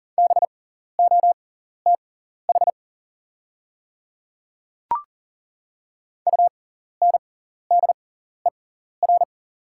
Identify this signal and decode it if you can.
Morse code, heard as a keyed tone, about 40 wpm, spaced as for 14 wpm, 690 Hz. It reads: BOTH UNDER
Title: 6x Character Spacing - Top 300 Words - 40wpm